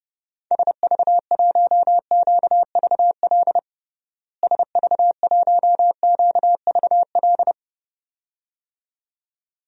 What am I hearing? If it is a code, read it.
Morse code, heard as a keyed tone, about 30 wpm, 700 Hz. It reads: SV1QVL SV1QVL
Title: DXCC-100 Call Signs - 6 Characters - Repeated 2x - 30wpm